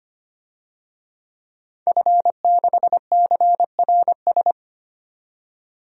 Morse code, heard as a keyed tone, about 25 wpm, 700 Hz. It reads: F6CRS
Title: DXCC-100 Call Signs - 5 Characters - 25wpm